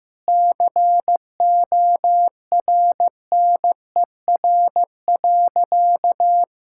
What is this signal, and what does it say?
Morse code, heard as a keyed tone, about 15 wpm, 700 Hz. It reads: CORNER.
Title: Sentences from Top 1600 Words 15wpm